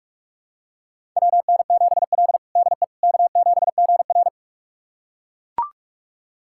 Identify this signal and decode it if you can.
Morse code, heard as a keyed tone, about 41 wpm, 700 Hz. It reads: WN7L DE K7GR
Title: USA Callsign DE Callsign - 4 Characters - 45wpm